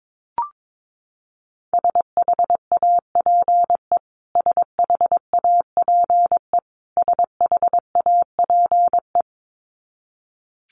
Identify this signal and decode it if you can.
Morse code, heard as a keyed tone, about 22 wpm, 700 Hz. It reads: SHAPE SHAPE SHAPE
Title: Top 500 Words Repeated 22wpm